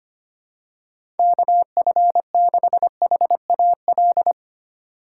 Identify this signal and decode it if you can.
Morse code, heard as a keyed tone, about 25 wpm, 700 Hz. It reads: KF6HAL
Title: USA Call Signs - 6 Characters - 4x Character Spacing - 25wpm